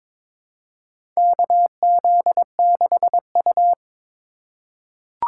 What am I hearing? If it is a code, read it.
Morse code, heard as a keyed tone, about 22 wpm, 700 Hz. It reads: KZ6U
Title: USA Call Signs - 4 Characters - 6x Character Spacing - 22wpm